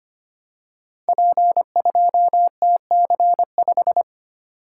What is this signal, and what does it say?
Morse code, heard as a keyed tone, about 25 wpm, 700 Hz. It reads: P2TC5
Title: Code Groups - 5 Letter:Numbers - 25wpm